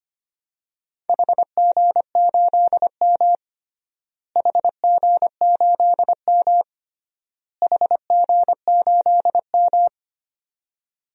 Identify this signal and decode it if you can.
Morse code, heard as a keyed tone, about 25 wpm, 700 Hz. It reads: HG8M HG8M HG8M
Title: DXCC-100 Call Signs - 4 Characters - Repeated 3x - 25wpm